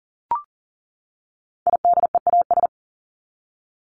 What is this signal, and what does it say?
Morse code, heard as a keyed tone, about 40 wpm, 700 Hz. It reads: IDEAS
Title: ES:S Suffix - Top 100 Words with Base Words 40wpm